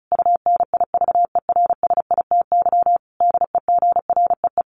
Morse code, heard as a keyed tone, about 34 wpm, 700 Hz. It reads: UNIVERSITY DEGREE
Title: Sentences from Top 1500 Words 35wpm